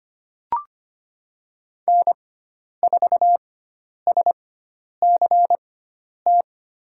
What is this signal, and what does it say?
Morse code, heard as a keyed tone, about 25 wpm, spaced as for 10 wpm, 700 Hz. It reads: N4SCT
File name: Lesson 16g - Letter c - Call Signs 25wpm